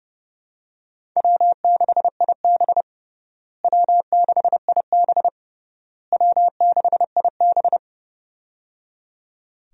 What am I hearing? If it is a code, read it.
Morse code, heard as a keyed tone, about 30 wpm, 700 Hz. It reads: W6IB W6IB W6IB
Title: USA Call Signs - 4 Characters - Repeated 3x - 30wpm